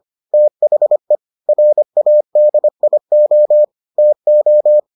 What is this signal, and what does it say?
Morse code, heard as a keyed tone, about 25 wpm, 600 Hz. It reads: THE RADIO TO